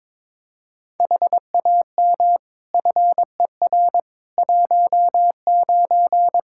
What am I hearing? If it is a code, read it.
Morse code, heard as a keyed tone, about 22 wpm, 700 Hz. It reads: HAM FER 19
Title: Practice QSO - Exchange 3 - More Info - 22wpm